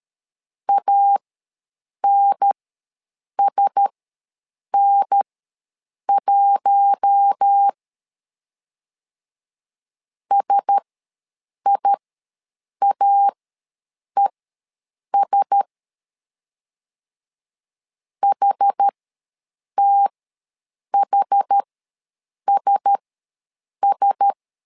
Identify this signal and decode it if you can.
Morse code, heard as a keyed tone, about 13 wpm, spaced as for 7 wpm, 800 Hz. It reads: ANSN1 SIAES HTHSS
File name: LEZIONI / 4 Q1S ws 12wpm 1827